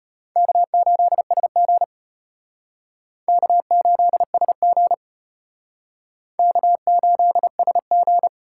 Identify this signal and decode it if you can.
Morse code, heard as a keyed tone, about 33 wpm, 700 Hz. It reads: K8SG K8SG K8SG